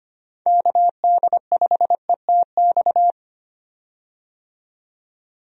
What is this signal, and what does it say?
Morse code, heard as a keyed tone, about 25 wpm, 700 Hz. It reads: KD5ETX
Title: USA Call Signs - 6 Characters - 25wpm